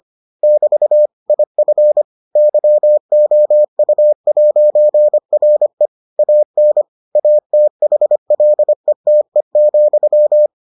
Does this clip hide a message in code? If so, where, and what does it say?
Yes: Morse code from 0.4 to 10.5 s, heard as a keyed tone, about 25 wpm, 600 Hz. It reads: <BT> IF YOU'RE AN ATHLETE,